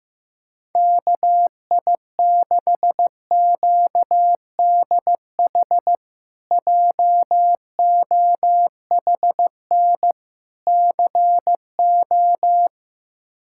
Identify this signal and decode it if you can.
Morse code, heard as a keyed tone, about 15 wpm, 700 Hz. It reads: KI6QDH JOHN CO
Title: K1USN SST QSO Practice - 15wpm